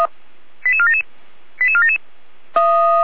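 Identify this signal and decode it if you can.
Five-tone selective call (ZVEI), heard as quick runs of five tones, tones of about 70 ms. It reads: 88477 88477
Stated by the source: ZVEI five-tone selective call received on an SDR, from fire brigade Feuerwehr Orlamünde.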